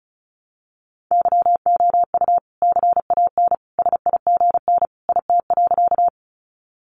Morse code, heard as a keyed tone, about 34 wpm, 700 Hz. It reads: YOU CAN SIGN IT.
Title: Sentences from Top 1000 Words 35wpm